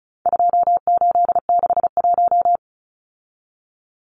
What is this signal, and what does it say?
Morse code, heard as a keyed tone, about 34 wpm, 700 Hz. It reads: 2861
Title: Numbers - 4-Digit 35wpm